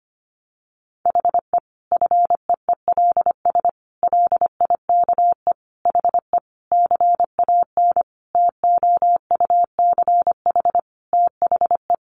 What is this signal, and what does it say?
Morse code, heard as a keyed tone, about 25 wpm, 700 Hz. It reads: HE FEELS LIKE HE CAN TOUCH THE
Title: Additional Easy Phrases 25wpm